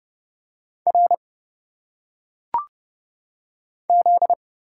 Morse code, heard as a keyed tone, about 29 wpm, 700 Hz. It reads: R Z
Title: Single Letters - Phonetic Pronunciation 30wpm